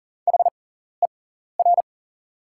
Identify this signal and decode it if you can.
Morse code, heard as a keyed tone, about 40 wpm, spaced as for 14 wpm, 690 Hz. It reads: HER
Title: 6x Character Spacing - Top 100 Words - 40wpm